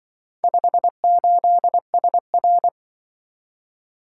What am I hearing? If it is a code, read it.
Morse code, heard as a keyed tone, about 24 wpm, 700 Hz. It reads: 58SR
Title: Speed-Racing - Four Letter-Number Combinations - 20wpm